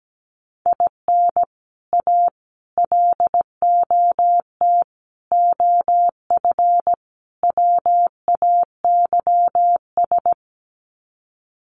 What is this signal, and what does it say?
Morse code, heard as a keyed tone, about 17 wpm, 700 Hz. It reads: IN A LOT OF WAYS
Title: Sets of 5 Words Encore 17wpm